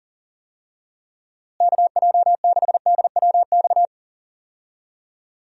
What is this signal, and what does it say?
Morse code, heard as a keyed tone, about 40 wpm, 700 Hz. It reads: KJ6DWX
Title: USA Call Signs - 6 Characters - Repeated 3x - 40wpm